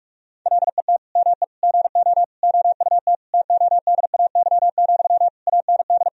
Morse code, heard as a keyed tone, about 46 wpm, 700 Hz. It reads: LET ME GO OUT TODAY, AND
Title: Sentences from Top 1000 Words - Review Part 4 - 45wpm